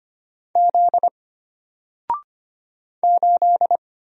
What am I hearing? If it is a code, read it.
Morse code, heard as a keyed tone, about 25 wpm, 700 Hz. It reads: Z 8
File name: Single-Letter-Number-25wpm